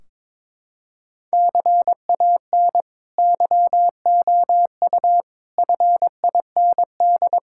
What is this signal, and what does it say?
Morse code, heard as a keyed tone, about 22 wpm, 700 Hz. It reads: CAN YOU FIND
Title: Sentences from Top 1600 Words 22wpm